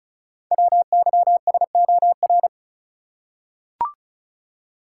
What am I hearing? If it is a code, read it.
Morse code, heard as a keyed tone, about 36 wpm, 690 Hz. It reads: WYSOR
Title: Code Groups - 5 Letters - 35wpm